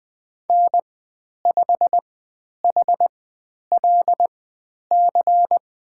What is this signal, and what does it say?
Morse code, heard as a keyed tone, about 20 wpm, spaced as for 10 wpm, 700 Hz. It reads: N5HLC
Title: 095 Lesson 16g - Letter c - Call Signs 20wpm